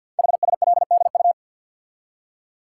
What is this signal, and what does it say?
Morse code, heard as a keyed tone, about 52 wpm, 700 Hz. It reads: HSLDU
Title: Code Groups - 5 Letter:Numbers - 50wpm